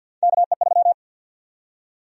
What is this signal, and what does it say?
Morse code, heard as a keyed tone, about 51 wpm, 700 Hz. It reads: KE3
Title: USA Call Sign Prefixes - 50wpm